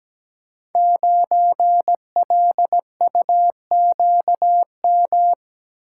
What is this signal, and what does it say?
Morse code, heard as a keyed tone, about 17 wpm, 700 Hz. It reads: 9LUQM
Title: Code Groups - 5 Letter:Numbers - 17wpm